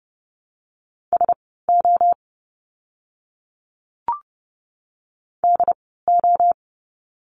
Morse code, heard as a keyed tone, about 30 wpm, spaced as for 17 wpm, 700 Hz. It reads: SO DO
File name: course / 30:10wpm / 230 Lesson 37f - Letter x - Words 30wpm